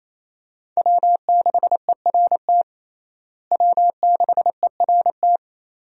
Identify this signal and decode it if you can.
Morse code, heard as a keyed tone, about 28 wpm, 700 Hz. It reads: W6ERT W6ERT
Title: USA Call Signs - 5 Characters - Repeated 2x - 28wpm